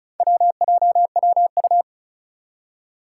Morse code, heard as a keyed tone, about 36 wpm, 700 Hz. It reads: WJWU
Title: Four Challenging Letter-Number Combinations 35wpm